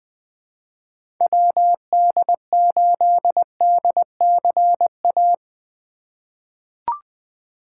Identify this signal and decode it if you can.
Morse code, heard as a keyed tone, about 20 wpm, 700 Hz. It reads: WD8DCA
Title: USA Call Signs - 6 Characters - 2x Character Spacing - 20wpm